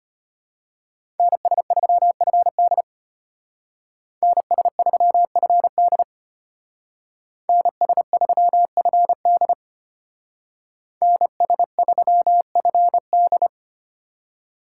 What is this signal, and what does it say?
Morse code, heard as a keyed tone, about 30 wpm, 700 Hz. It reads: NS3FD NS3FD NS3FD NS3FD
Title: Speed Racing - USA 2x2 Call Signs 25wpm